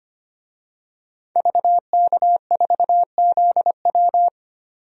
Morse code, heard as a keyed tone, about 25 wpm, 700 Hz. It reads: VK4ZW
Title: DXCC-100 Call Signs - 5 Characters - 25wpm